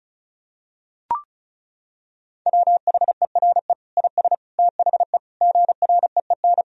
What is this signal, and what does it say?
Morse code, heard as a keyed tone, about 36 wpm, 700 Hz. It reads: WHERE IS THE GREEN
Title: Sentences from Top 1000 Words - Review Part 2 - 35wpm